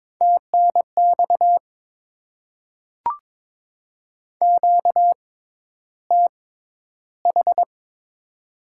Morse code, heard as a keyed tone, about 22 wpm, spaced as for 11 wpm, 690 Hz. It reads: TNX QTH
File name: Top 20 QSO Elements - 6x Character Spacing - 22wpm